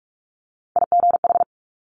Morse code, heard as a keyed tone, about 46 wpm, 700 Hz. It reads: IGH
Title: Multi-letter Phonemes Quick Fire 45wpm